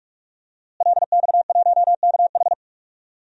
Morse code, heard as a keyed tone, about 45 wpm, 700 Hz. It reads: LX1KH
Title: DXCC-100 Call Signs - 5 Characters - Repeated 3x - 45wpm